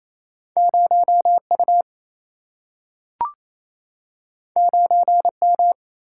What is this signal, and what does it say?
Morse code, heard as a keyed tone, about 28 wpm, 700 Hz. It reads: 0 U 9 M
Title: Single-Letter Single-Number 28wpm